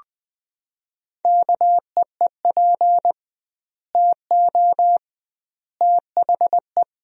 Morse code, heard as a keyed tone, about 21 wpm, 700 Hz. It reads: KEEP TO THE
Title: 2x Word-Spacing - Sentences from Top 300 Words - 20wpm